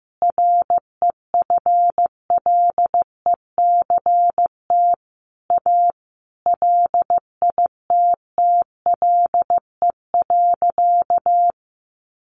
Morse code, heard as a keyed tone, about 14 wpm, 700 Hz. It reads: REFLECT A LITTLE.
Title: Sentences from Top 1500 Words 15wpm